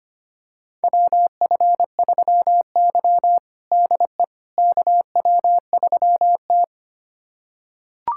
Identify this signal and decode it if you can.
Morse code, heard as a keyed tone, about 25 wpm, 700 Hz. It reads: WF3Y DE KW3T
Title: USA Callsign DE Callsign - 4 Characters - 25wpm